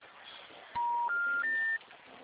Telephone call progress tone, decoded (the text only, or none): special information tone